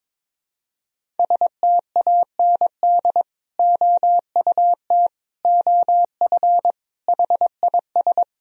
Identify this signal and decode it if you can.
Morse code, heard as a keyed tone, about 22 wpm, 700 Hz. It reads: STAND OUT OF HIS
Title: Sentences from Top 1000 Words - Review Part 1 - 22wpm